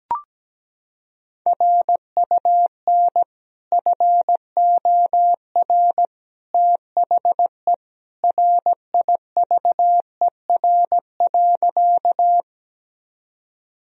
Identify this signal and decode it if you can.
Morse code, heard as a keyed tone, about 17 wpm, 700 Hz. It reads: RUN FOR THE RIVER.
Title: Sentences from Top 1000 Words - Review Part 1 - 17wpm